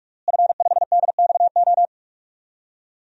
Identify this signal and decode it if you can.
Morse code, heard as a keyed tone, about 46 wpm, 700 Hz. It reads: F5DXO
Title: DXCC-100 Call Signs - 5 Characters - Repeated 3x - 45wpm